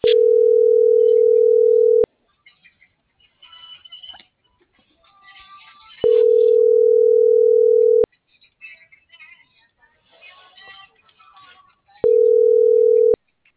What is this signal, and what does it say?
Ringback tone (North America), 440 + 480 Hz.
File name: CENTRAL BLN / out ximena.diaz - .63097